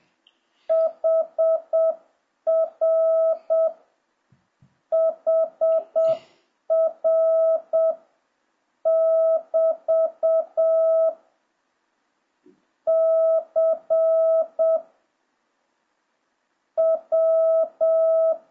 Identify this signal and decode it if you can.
Morse code, heard as a keyed tone, about 7 wpm, 650 Hz. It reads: HR HR <BT> C W